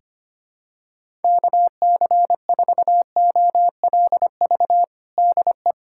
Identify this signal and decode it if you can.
Morse code, heard as a keyed tone, about 25 wpm, 700 Hz. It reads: KC4OLV DE
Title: USA Callsign DE Callsign - 6 Characters - 25wpm